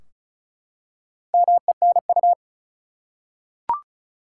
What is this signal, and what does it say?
Morse code, heard as a keyed tone, about 34 wpm, 700 Hz. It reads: MENU
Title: Four Letter Words - 35wpm